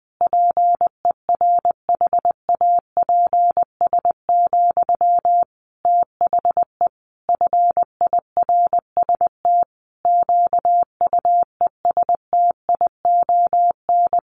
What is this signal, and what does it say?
Morse code, heard as a keyed tone, about 19 wpm, 700 Hz. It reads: PERHAPS, THE FIRST QUESTION
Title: Sentences from Top 1500 Words 20wpm